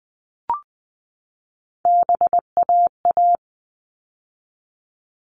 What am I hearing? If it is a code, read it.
Morse code, heard as a keyed tone, about 20 wpm, 700 Hz. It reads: BAA
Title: All Cut Numbers - 3-Digit 20wpm